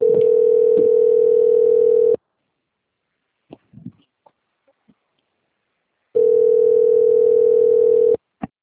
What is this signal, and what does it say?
Ringback tone (North America), 440 + 480 Hz.